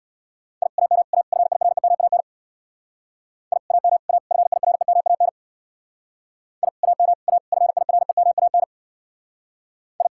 Morse code, heard as a keyed tone, about 65 wpm, 700 Hz. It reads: I AM A FIREMAN I AM A FIREMAN I AM A FIREMAN I